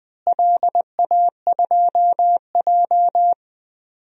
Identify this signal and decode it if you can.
Morse code, heard as a keyed tone, about 20 wpm, 700 Hz. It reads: LA2J
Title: DXCC-100 Call Signs - 4 Characters - Repeated 3x - 20wpm